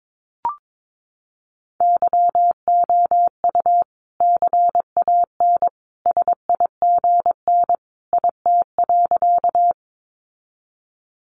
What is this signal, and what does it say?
Morse code, heard as a keyed tone, about 22 wpm, 700 Hz. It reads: YOU CAN SIGN IT.
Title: Sentences from Top 1000 Words 22wpm